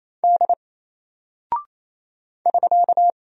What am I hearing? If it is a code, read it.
Morse code, heard as a keyed tone, about 27 wpm, 700 Hz. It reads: D <SK>
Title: Single Character (Letter:Number:Punctuation:Prosign) 28wpm